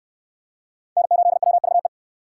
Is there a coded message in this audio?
Yes: Morse code from 1.0 to 1.9 s, heard as a keyed tone, about 68 wpm, 700 Hz. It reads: N8WVE